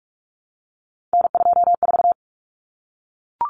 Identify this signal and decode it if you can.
Morse code, heard as a keyed tone, about 44 wpm, 700 Hz. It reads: N24
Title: Common Cut Numbers - 3-Digit 45wpm